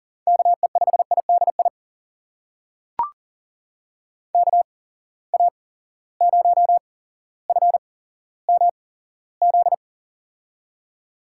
Morse code, heard as a keyed tone, about 40 wpm, spaced as for 17 wpm, 700 Hz. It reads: KE5IDI KA0FMZ
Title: USA Call Signs - 6 Characters - 8x Character Spacing - 40wpm